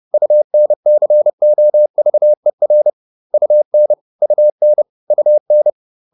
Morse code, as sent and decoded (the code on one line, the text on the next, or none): ..- -. -.-. --- ...- . .-. / ..- -. / ..- -. / ..- -.
UNCOVER UN UN UN